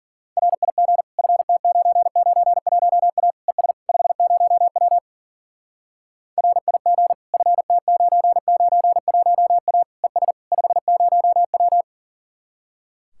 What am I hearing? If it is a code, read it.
Morse code, heard as a keyed tone, about 52 wpm, 700 Hz. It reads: RIG FT991A ES 50W RIG FT991A ES 50W